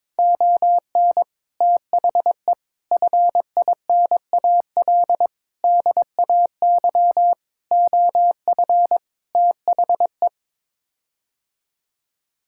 Morse code, heard as a keyed tone, about 22 wpm, 700 Hz. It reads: ON THE FINAL DAY OF THE
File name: practice / 22wpm / Sets of 6 Words - 22wpm